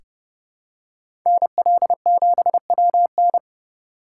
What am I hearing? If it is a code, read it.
Morse code, heard as a keyed tone, about 32 wpm, 700 Hz. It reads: NL7WN
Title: USA Call Signs - 5 Characters - 2x Character Spacing - 30wpm